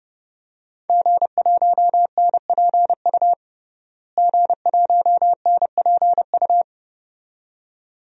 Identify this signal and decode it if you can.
Morse code, heard as a keyed tone, about 30 wpm, 700 Hz. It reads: G1NPU G1NPU
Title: DXCC-100 Call Signs - 5 Characters - Repeated 2x - 30wpm